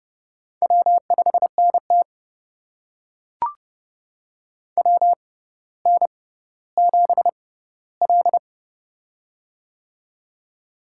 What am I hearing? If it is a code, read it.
Morse code, heard as a keyed tone, about 30 wpm, spaced as for 15 wpm, 700 Hz. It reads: W5NT WN7L
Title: USA Call Signs - 4 Characters - 6x Character Spacing - 30wpm